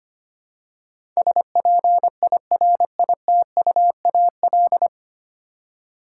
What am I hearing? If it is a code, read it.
Morse code, heard as a keyed tone, about 25 wpm, 690 Hz. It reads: SPIRITUAL
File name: practice / 25wpm / 9 Letter Words - First 3 Letters Given - 25wpm